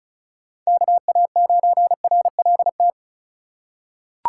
Morse code, heard as a keyed tone, about 36 wpm, 700 Hz. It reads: KA9RLT
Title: USA Call Signs - 6 Characters - 4x Character Spacing - 35wpm